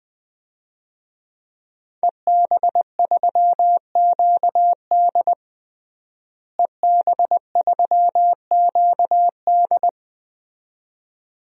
Morse code, heard as a keyed tone, about 20 wpm, 700 Hz. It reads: EB3QD EB3QD
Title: DXCC-100 Call Signs - 5 Characters - Repeated 2x - 20wpm